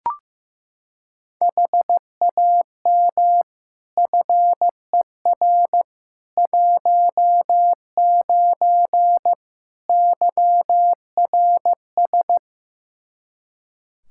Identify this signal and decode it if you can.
Morse code, heard as a keyed tone, about 15 wpm, 700 Hz. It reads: HAM FER 19 YRS
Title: Practice QSO - Exchange 3 - More Info - 15wpm